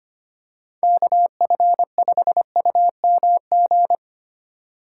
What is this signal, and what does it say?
Morse code, heard as a keyed tone, about 25 wpm, 700 Hz. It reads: KF5UMG
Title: USA Call Signs - 6 Characters - 4x Character Spacing - 25wpm